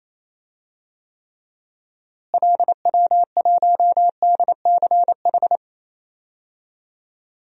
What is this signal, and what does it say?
Morse code, heard as a keyed tone, about 28 wpm, 700 Hz. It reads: LW1DCH
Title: DXCC-100 Call Signs - 6 Characters - Repeated 2x - 28wpm